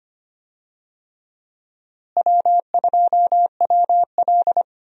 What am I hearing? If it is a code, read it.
Morse code, heard as a keyed tone, about 25 wpm, 700 Hz. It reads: W2WL
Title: USA Call Signs - 4 Characters - Repeated 3x - 25wpm